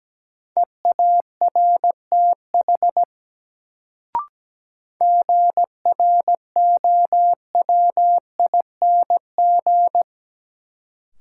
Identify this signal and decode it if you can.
Morse code, heard as a keyed tone, about 17 wpm, 700 Hz. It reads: EARTH GROWING
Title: Sets of 5 Words Encore - New Word Drill - 17wpm